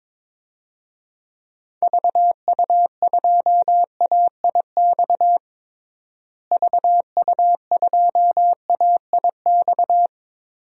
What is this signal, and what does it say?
Morse code, heard as a keyed tone, about 22 wpm, 700 Hz. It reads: VU2AIX VU2AIX
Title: DXCC-100 Call Signs - 6 Characters - Repeated 2x - 22wpm